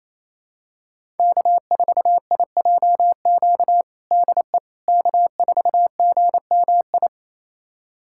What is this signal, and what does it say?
Morse code, heard as a keyed tone, about 28 wpm, 700 Hz. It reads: K4IJQ DE K4GMI
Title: USA Callsign DE Callsign 28wpm